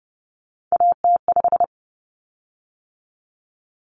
Morse code, heard as a keyed tone, about 30 wpm, 700 Hz. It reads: AT5
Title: Common Cut Numbers - 3-Digit 30wpm